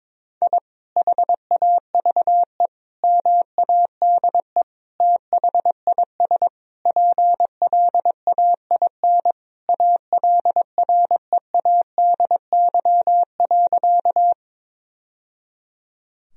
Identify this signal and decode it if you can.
Morse code, heard as a keyed tone, about 22 wpm, 700 Hz. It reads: I HAVE MADE THIS PLAIN ALREADY.
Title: Sentences from Top 1000 Words - Review Part 2 - 22wpm